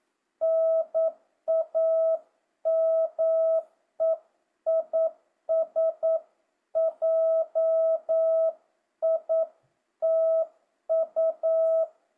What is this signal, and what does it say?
Morse code, heard as a keyed tone, about 9 wpm, 650 Hz. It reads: NAME IS JITU
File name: jh8cbh / 2025cw